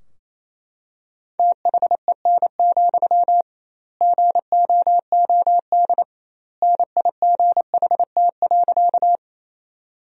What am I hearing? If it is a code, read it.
Morse code, heard as a keyed tone, about 28 wpm, 700 Hz. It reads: THEN, GOOD NIGHT.
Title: 2x Word-Spacing - Sentences from Top 300 Words - 28wpm